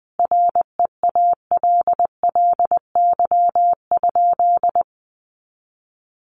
Morse code, heard as a keyed tone, about 20 wpm, 700 Hz. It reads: REALLY?
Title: Sentences from Top 600 Words 20wpm